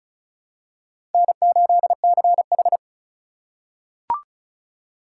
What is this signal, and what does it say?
Morse code, heard as a keyed tone, about 36 wpm, 700 Hz. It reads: N8CH
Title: USA Call Signs - 4 Characters - 3x Character Spacing - 35wpm